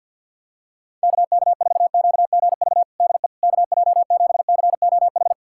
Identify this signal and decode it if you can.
Morse code, heard as a keyed tone, about 49 wpm, 690 Hz. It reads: KK4QGV DE KJ7COH
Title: USA Callsign DE Callsign 50wpm